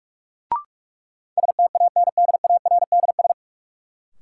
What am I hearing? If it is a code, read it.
Morse code, heard as a keyed tone, about 46 wpm, 700 Hz. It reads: STANDARDS
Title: Sentences from Top 1000 Words - Review Part 6 - New Word Drill - 45wpm